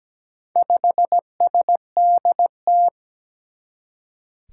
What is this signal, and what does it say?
Morse code, heard as a keyed tone, about 17 wpm, 700 Hz. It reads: 5SDT
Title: Four Challenging Letter-Number Combinations 17wpm